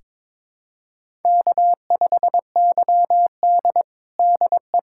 Morse code, heard as a keyed tone, about 23 wpm, 700 Hz. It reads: K5YD DE
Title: USA Callsign DE Callsign - 4 Characters - 22wpm